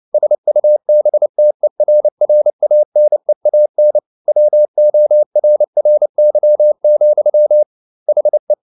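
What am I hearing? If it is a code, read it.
Morse code, heard as a keyed tone, about 29 wpm, 600 Hz. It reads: SUBTERRANEAN WORRY, HE
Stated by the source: Morse code at 29 wpm, sending Ken Follett's novel Never in English, from CW med Gnister.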